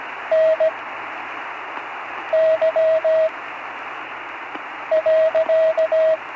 Morse code, heard as a keyed tone, about 17 wpm, spaced as for 5 wpm, 640 Hz, with radio noise. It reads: NY.
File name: CW Learning Programs / Learning Morse Code (from Tune in the World) / 43 on the air